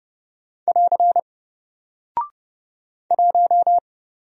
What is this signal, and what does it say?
Morse code, heard as a keyed tone, about 30 wpm, 700 Hz. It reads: <AR> 1
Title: Single Character (Letter:Number:Punctuation:Prosign) Alt 30wpm